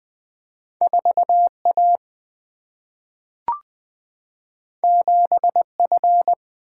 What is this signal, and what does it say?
Morse code, heard as a keyed tone, about 20 wpm, 700 Hz. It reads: 4 A 7 F
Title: Two Letter-Number Combinations - 20wpm